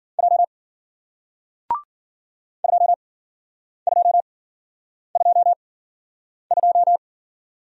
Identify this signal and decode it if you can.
Morse code, heard as a keyed tone, about 55 wpm, spaced as for 9 wpm, 700 Hz. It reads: J 2222